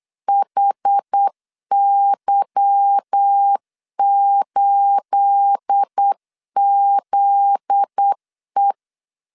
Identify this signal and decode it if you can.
Morse code, heard as a keyed tone, about 8 wpm, 800 Hz. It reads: HY8ZE
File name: LEZIONI / 10 SIGNS / 10C DOT 8wpm 3649